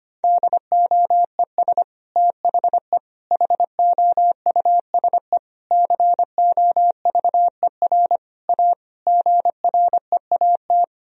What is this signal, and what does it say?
Morse code, heard as a keyed tone, about 25 wpm, 700 Hz. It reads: DOES THE HOUSE COVER A GREAT
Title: Sentences from Top 1000 Words - Review Part 2 - 25wpm